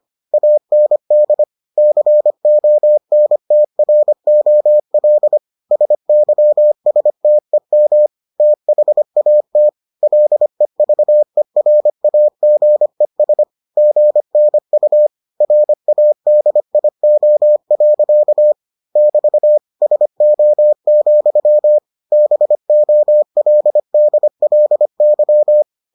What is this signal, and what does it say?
Morse code, heard as a keyed tone, about 25 wpm, 600 Hz. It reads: AND CONTROL SYSTEM THAT LEVERAGES GNU RADIO. <BT> SO, BOLDLY